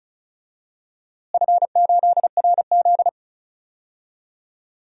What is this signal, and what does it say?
Morse code, heard as a keyed tone, about 36 wpm, 700 Hz. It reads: F8RZ
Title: DXCC-100 Call Signs - 4 Characters - Repeated 3x - 35wpm